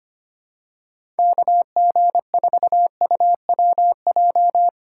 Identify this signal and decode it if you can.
Morse code, heard as a keyed tone, about 25 wpm, 700 Hz. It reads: KG4UWJ
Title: USA Call Signs - 6 Characters - 6x Character Spacing - 25wpm